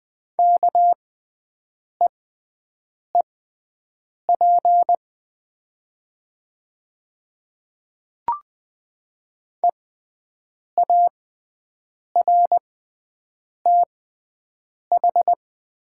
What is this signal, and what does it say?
Morse code, heard as a keyed tone, about 20 wpm, spaced as for 7 wpm, 700 Hz. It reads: KEEP EARTH
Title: 6x Character Spacing - Top 500 Words - 20wpm